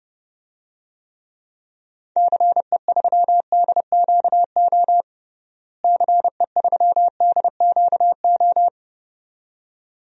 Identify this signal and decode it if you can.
Morse code, heard as a keyed tone, about 30 wpm, 700 Hz. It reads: CE3DQO CE3DQO
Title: DXCC-100 Call Signs - 6 Characters - Repeated 2x - 30wpm